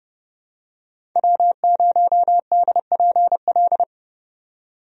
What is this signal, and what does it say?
Morse code, heard as a keyed tone, about 32 wpm, 700 Hz. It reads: W0DPL
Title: USA Call Signs - 5 Characters - 30wpm